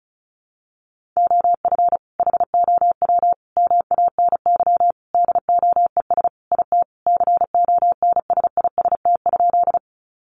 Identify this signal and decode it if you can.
Morse code, heard as a keyed tone, about 34 wpm, 700 Hz. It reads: OF HOW MANY DOES IT CONSIST?
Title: Sentences from Top 1400 Words 35wpm